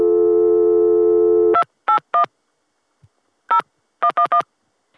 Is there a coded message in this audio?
Yes: DTMF from 1.5 to 4.4 s, heard as telephone touch tones, tones of about 90 ms. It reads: 281 0111